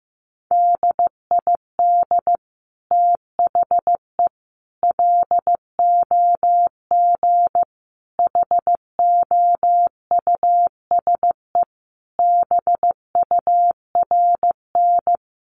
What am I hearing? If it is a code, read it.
Morse code, heard as a keyed tone, about 15 wpm, 700 Hz. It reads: DID THE LOG HOUSE BURN
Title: Sentences from Top 1400 Words 15wpm